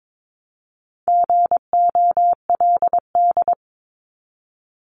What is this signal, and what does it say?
Morse code, heard as a keyed tone, about 22 wpm, 700 Hz. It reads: GOLD
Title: Multi-letter Phonemes Repeated 3x 22wpm